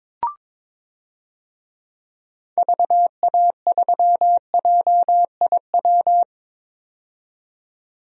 Morse code, heard as a keyed tone, about 22 wpm, 700 Hz. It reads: VA3JIW
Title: DXCC-100 Call Signs - 6 Characters - 22wpm